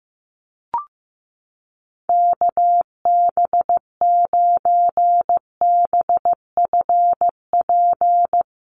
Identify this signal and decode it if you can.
Morse code, heard as a keyed tone, about 15 wpm, 700 Hz. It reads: KB9BFP
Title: USA Call Signs Repeated 3x 15wpm